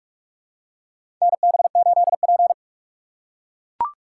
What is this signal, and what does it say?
Morse code, heard as a keyed tone, about 44 wpm, 700 Hz. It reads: NB8P
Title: USA Call Signs - 4 Characters - 2x Character Spacing - 45wpm